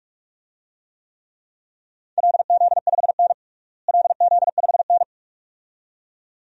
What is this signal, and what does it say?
Morse code, heard as a keyed tone, about 46 wpm, 700 Hz. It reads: LZ5N LZ5N
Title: DXCC-100 Call Signs - 4 Characters - Repeated 2x - 45wpm